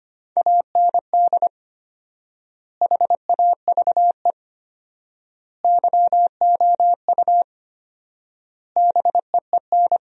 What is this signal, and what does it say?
Morse code, heard as a keyed tone, about 25 wpm, 700 Hz. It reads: AND HAVE YOU BEEN
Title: 4x Word-Spacing - Sentences from Top 200 Words - 25wpm